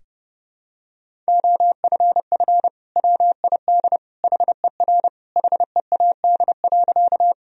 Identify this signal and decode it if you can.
Morse code, heard as a keyed tone, about 32 wpm, 700 Hz. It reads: OFF WID HER HEAD.
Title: Sentences from Top 300 Words and Abbreviations 30wpm